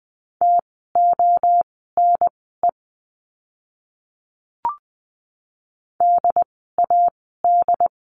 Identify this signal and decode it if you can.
Morse code, heard as a keyed tone, about 20 wpm, spaced as for 15 wpm, 700 Hz. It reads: TONE DAD
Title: 068 Lesson 12f - Letter d - Words 20wpm